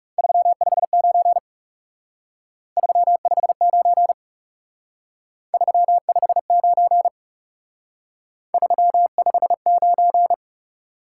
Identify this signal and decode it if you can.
Morse code, heard as a keyed tone, about 38 wpm, 700 Hz. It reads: 359 359 359 359
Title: Speed-Racing Top 180 QSO Elements 30wpm